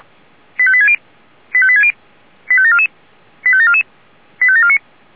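Five-tone selective call (ZVEI), heard as quick runs of five tones, tones of about 70 ms. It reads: 85860 85860 85744 85744 85749
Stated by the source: Five-tone ZVEI selective call received on an SDR.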